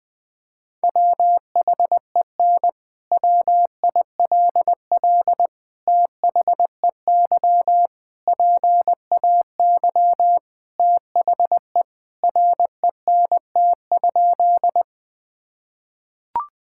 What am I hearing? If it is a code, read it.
Morse code, heard as a keyed tone, about 20 wpm, 700 Hz. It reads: WHEN WILL THEY PAY THE RENT?
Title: Sentences from Top 1000 Words - Review Part 5 - 20wpm